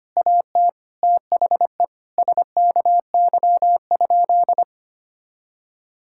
Morse code, heard as a keyed tone, about 25 wpm, 700 Hz. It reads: AT THE SKY?
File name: practice / 25wpm / Sentences with 4-Word Phrases - 25wpm